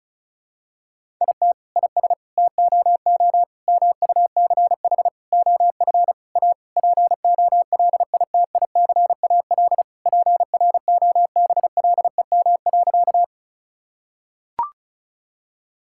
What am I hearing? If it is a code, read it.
Morse code, heard as a keyed tone, about 34 wpm, 700 Hz. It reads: IT IS TOO MUCH OF A POLITICAL PROBLEM.
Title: Sentences from Top 1000 Words - Review Part 5 - 35wpm